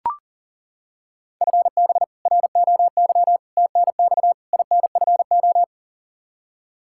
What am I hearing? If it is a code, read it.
Morse code, heard as a keyed tone, about 40 wpm, 700 Hz. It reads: FB ROY TNX INFO
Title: K1USN SST QSO Practice 40wpm